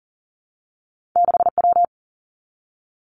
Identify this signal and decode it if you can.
Morse code, heard as a keyed tone, about 40 wpm, 700 Hz. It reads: 6W
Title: Two Challenging Letter-Number Combinations 40wpm